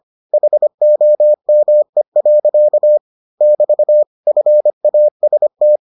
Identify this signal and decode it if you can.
Morse code, heard as a keyed tone, about 25 wpm, 600 Hz. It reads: HOME. <BT> FAST